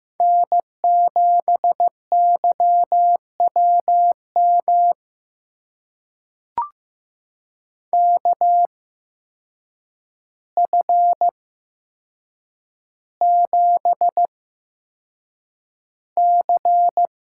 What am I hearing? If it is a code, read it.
Morse code, heard as a keyed tone, about 15 wpm, spaced as for 7 wpm, 700 Hz. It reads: N7YWM KF7C
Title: USA Call Signs - 5 Characters - 8x Character Spacing - 15wpm